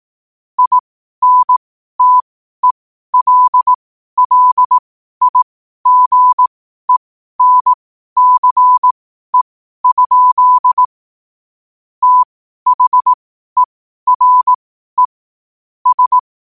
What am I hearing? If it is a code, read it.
Morse code, heard as a keyed tone, about 18 wpm, spaced as for 12 wpm, 1000 Hz. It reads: INTELLIGENCE? THERE S
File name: qotdfiles / 12wpm